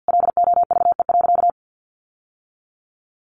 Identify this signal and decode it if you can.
Morse code, heard as a keyed tone, about 48 wpm, 700 Hz. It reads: LOVE.
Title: Sentences from Top 1100 Words 50wpm